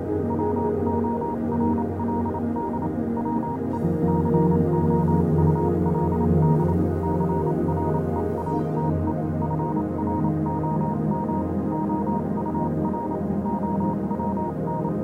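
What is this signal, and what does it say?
Morse code, heard as a keyed tone, about 28 wpm, spaced as for 25 wpm, 940 Hz, with noise. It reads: WORLD WE COULD CREATE FROM OUR OWN
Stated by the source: Morse code version of a music track.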